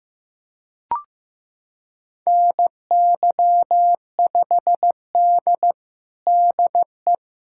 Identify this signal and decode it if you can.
Morse code, heard as a keyed tone, about 15 wpm, 700 Hz. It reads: NY5D DE
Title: USA Callsign DE Callsign 15wpm